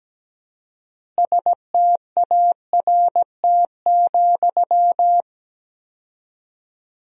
Morse code, heard as a keyed tone, about 17 wpm, 700 Hz. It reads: START,
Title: 4x Word-Spacing - Sentences from Top 300 Words - 17wpm